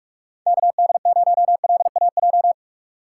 Morse code, heard as a keyed tone, about 41 wpm, 700 Hz. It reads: KD0LAJ
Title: USA Call Signs - 3x Character Spacing - 45wpm